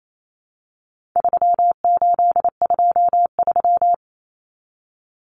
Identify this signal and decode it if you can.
Morse code, heard as a keyed tone, about 28 wpm, 700 Hz. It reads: 3823
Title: Common Cut Numbers - 4-Digit 28wpm